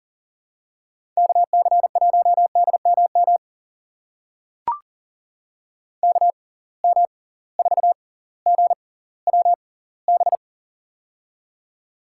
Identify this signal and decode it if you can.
Morse code, heard as a keyed tone, about 40 wpm, spaced as for 20 wpm, 700 Hz. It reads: KC1DMM KM4GWB
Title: USA Call Signs - 6x Character Spacing - 40wpm